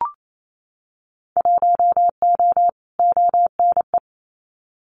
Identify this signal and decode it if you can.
Morse code, heard as a keyed tone, about 28 wpm, 700 Hz. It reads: 1O ONE
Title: Winter Field Day Exchanges 28wpm